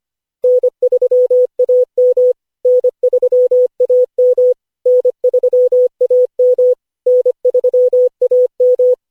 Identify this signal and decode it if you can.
Morse code, heard as a keyed tone, about 25 wpm, 500 Hz. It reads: N3AM N3AM N3AM N3AM
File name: cw / Session5 cw training